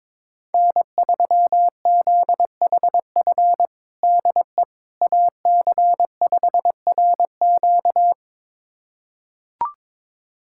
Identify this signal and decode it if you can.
Morse code, heard as a keyed tone, about 22 wpm, 700 Hz. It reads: N3ZHF DE AC5RQ
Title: USA Callsign DE Callsign - 5 Characters - 22wpm